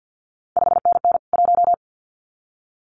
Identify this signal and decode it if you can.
Morse code, heard as a keyed tone, about 50 wpm, 700 Hz. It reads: 5NN 1
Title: CQ WW DX Contest Exchange - 50wpm